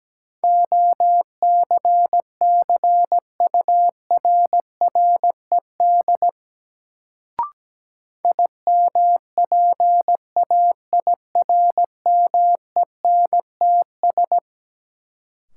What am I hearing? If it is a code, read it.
Morse code, heard as a keyed tone, about 17 wpm, 700 Hz. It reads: OCCURRED IMPAIRMENTS
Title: Sets of 5 Words Encore - New Word Drill - 17wpm